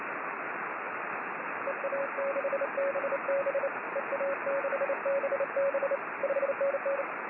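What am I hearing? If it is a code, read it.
Morse code, heard as a keyed tone, about 30 wpm, 550 Hz, with radio noise. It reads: EA6BB EA6BB 5NN
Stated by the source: Morse signal on the 6 m band.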